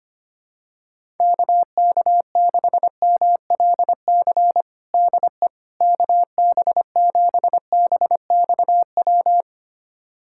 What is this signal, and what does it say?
Morse code, heard as a keyed tone, about 25 wpm, 700 Hz. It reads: KK6MLC DE KB7BXW
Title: USA Callsign DE Callsign 25wpm